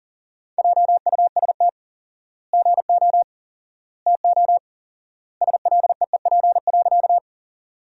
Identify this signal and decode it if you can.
Morse code, heard as a keyed tone, about 40 wpm, 700 Hz. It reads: JUST GO TO SLEEP.
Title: 4x Word-Spacing - Sentences from Top 500 Words - 40wpm